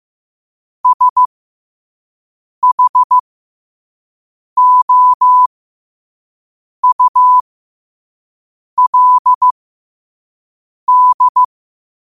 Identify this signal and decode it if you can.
Morse code, heard as a keyed tone, about 15 wpm, spaced as for 5 wpm, 1000 Hz. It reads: SHOULD
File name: qotdfiles / 5wpm